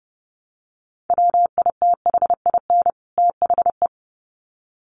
Morse code, heard as a keyed tone, about 30 wpm, 700 Hz. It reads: WITHIN THE
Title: Sets of 2 Words 30wpm